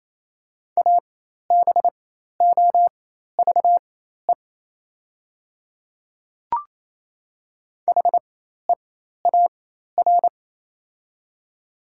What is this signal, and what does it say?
Morse code, heard as a keyed tone, about 28 wpm, spaced as for 13 wpm, 700 Hz. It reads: ABOVE HEAR
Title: 4x Character Spacing - Top 500 Words - 28wpm